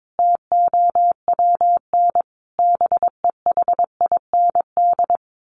Morse code, heard as a keyed tone, about 22 wpm, 700 Hz. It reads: TOWN BEHIND
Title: Sentences from Top 1300 Words 22wpm